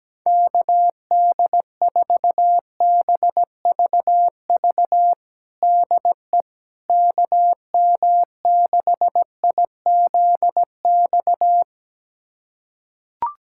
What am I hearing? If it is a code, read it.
Morse code, heard as a keyed tone, about 17 wpm, 700 Hz. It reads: KD4BVV DE KM6IZX
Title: USA Callsign DE Callsign - 6 Characters - 17wpm